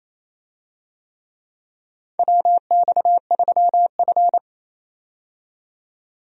Morse code, heard as a keyed tone, about 28 wpm, 700 Hz. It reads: WX3F
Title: USA Call Signs - 4 Characters - 28wpm